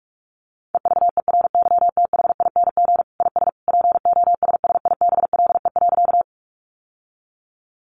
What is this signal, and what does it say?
Morse code, heard as a keyed tone, about 44 wpm, 700 Hz. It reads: EVERYTHING IS POSSIBLE.
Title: Sentences from Top 700 Words 45wpm